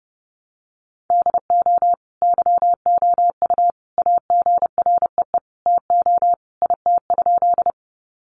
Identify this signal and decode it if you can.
Morse code, heard as a keyed tone, about 30 wpm, 700 Hz. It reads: DO YOU AGREE TO IT?
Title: Sentences from Top 1100 Words 30wpm